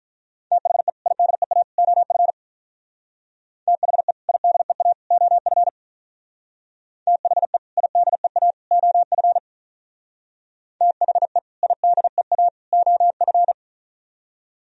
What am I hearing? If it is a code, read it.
Morse code, heard as a keyed tone, about 44 wpm, 700 Hz. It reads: THE IDEA OF THE IDEA OF THE IDEA OF THE IDEA OF